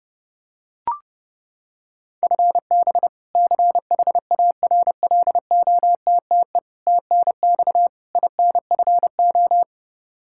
Morse code, heard as a keyed tone, about 31 wpm, 700 Hz. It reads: FB CHARLOTTE TNX INFO
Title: Practice QSO - Exchange 3 - More Info - 30wpm